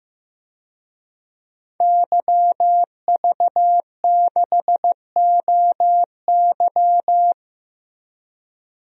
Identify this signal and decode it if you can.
Morse code, heard as a keyed tone, about 15 wpm, 700 Hz. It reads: YV6OY
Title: DXCC-100 Call Signs - 5 Characters - Repeated 2x - 15wpm